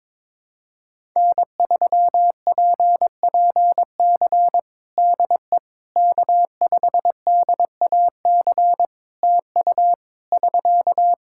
Morse code, heard as a keyed tone, about 22 wpm, 700 Hz. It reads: N3PPC DE K5DAC TU <SK>
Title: K1USN SST QSO Practice 22wpm